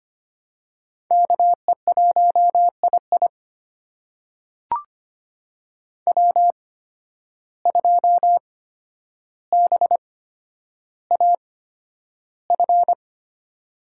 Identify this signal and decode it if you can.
Morse code, heard as a keyed tone, about 25 wpm, spaced as for 11 wpm, 700 Hz. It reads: KE1II W2BAF
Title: USA Call Signs - 5 Characters - 8x Character Spacing - 25wpm